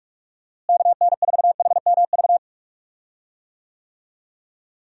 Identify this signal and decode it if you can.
Morse code, heard as a keyed tone, about 46 wpm, 700 Hz. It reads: KN4HMV
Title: USA Call Signs - 6 Characters - Repeated 3x - 45wpm